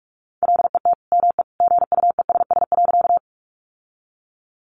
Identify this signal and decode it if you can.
Morse code, heard as a keyed tone, about 44 wpm, 700 Hz. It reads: LET ME GUESS.
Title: Sentences from Top 1500 Words 45wpm